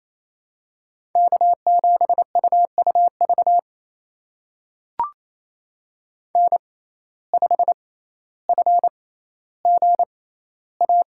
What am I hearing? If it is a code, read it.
Morse code, heard as a keyed tone, about 28 wpm, spaced as for 14 wpm, 700 Hz. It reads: K7UUV N5FGA
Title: USA Call Signs - 5 Characters - 6x Character Spacing - 28wpm